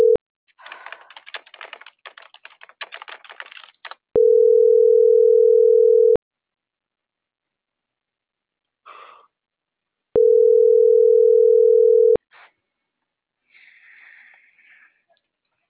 Ringback tone (North America), 440 + 480 Hz.
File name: CENTRAL MABE / out vanesa.mabe